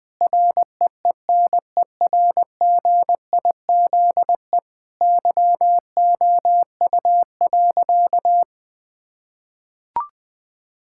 Morse code, heard as a keyed tone, about 20 wpm, 700 Hz. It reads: REENERGIZE YOU.
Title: Difficult Sentences with Hints - 20wpm